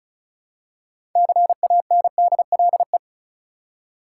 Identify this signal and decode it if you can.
Morse code, heard as a keyed tone, about 36 wpm, 700 Hz. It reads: CANDLE
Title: Six Letter Words - 35wpm